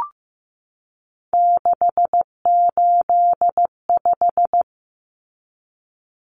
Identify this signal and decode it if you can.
Morse code, heard as a keyed tone, about 15 wpm, 700 Hz. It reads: 685
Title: Common Cut Numbers - 3-Digit 15wpm